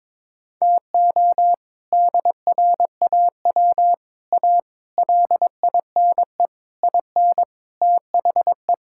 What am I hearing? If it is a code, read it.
Morse code, heard as a keyed tone, about 22 wpm, 700 Hz. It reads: TO DRAW A LINE IN THE
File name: practice / 22wpm / Sets of 7 Words - 22wpm